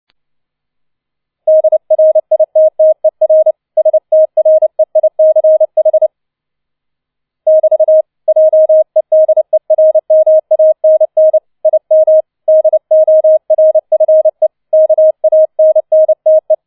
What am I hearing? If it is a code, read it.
Morse code, heard as a keyed tone, about 29 wpm, 630 Hz. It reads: DRITTER STREICH <BT> JEDERMANN IM DORFE KANNTE